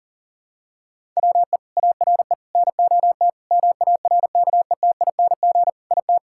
Morse code, heard as a keyed tone, about 40 wpm, 700 Hz. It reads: WE ARE NOT MARKETING IT
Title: Sentences from Top 1000 Words - Review Part 5 - 40wpm